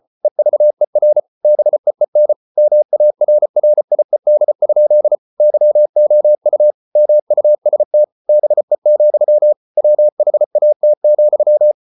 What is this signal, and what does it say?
Morse code, heard as a keyed tone, about 34 wpm, 600 Hz. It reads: EVER BEEN MARRIED? YOU MUST BE, WHAT,